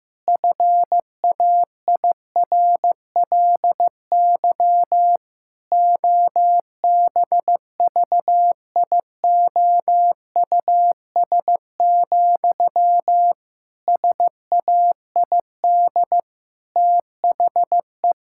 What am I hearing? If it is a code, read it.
Morse code, heard as a keyed tone, about 16 wpm, 700 Hz. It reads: FAIRLY OBVIOUS, SAID THE
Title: Sentences from Top 1600 Words 15wpm